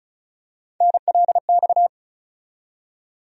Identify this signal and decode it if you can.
Morse code, heard as a keyed tone, about 34 wpm, 700 Hz. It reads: NLX
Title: USA Call Sign Postfixes - 3 Letters 35wpm